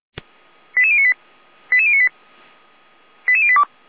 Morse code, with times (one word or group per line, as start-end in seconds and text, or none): none